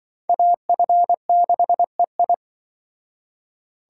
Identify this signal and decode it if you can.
Morse code, heard as a keyed tone, about 24 wpm, 700 Hz. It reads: AF6EI
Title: Speed Racing - USA 2x2 Call Signs 20wpm